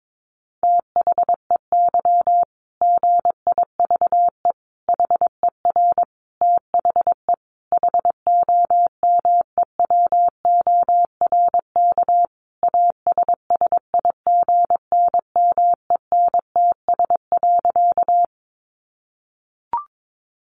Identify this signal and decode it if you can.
Morse code, heard as a keyed tone, about 22 wpm, 700 Hz. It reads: THEY GIVE HER THE HOMEWORK ASSIGNMENTS.
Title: Easy Phrases 22wpm